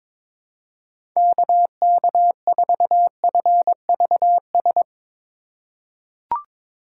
Morse code, heard as a keyed tone, about 22 wpm, 700 Hz. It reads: KK4FVS
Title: USA Call Signs - 6 Characters - 8x Character Spacing - 22wpm